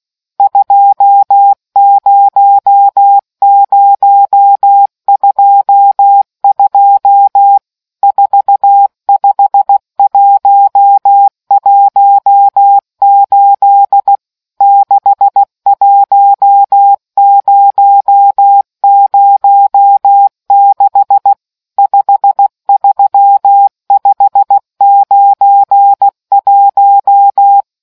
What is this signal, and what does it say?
Morse code, heard as a keyed tone, about 16 wpm, 800 Hz. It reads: 20022 45118 61006 53591